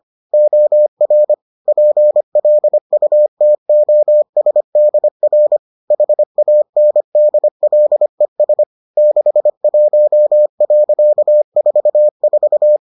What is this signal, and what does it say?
Morse code, heard as a keyed tone, about 25 wpm, 600 Hz. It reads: OR PLUTOSDR HANDLES 61.44